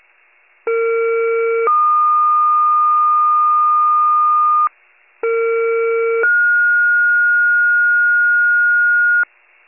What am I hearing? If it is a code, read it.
Two-tone page, heard as a one-second tone, then a three-second tone, about 460 then 1220 Hz, about 460 then 1430 Hz, with noise. It is heard from 0.7 to 9.2 s.
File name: pager / 57 FIRE / wcalert